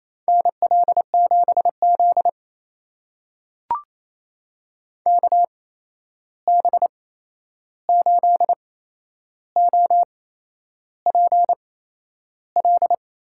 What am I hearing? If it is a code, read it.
Morse code, heard as a keyed tone, about 28 wpm, spaced as for 11 wpm, 700 Hz. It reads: NL7Z KB8OPL
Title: USA Call Signs - 8x Character Spacing - 28wpm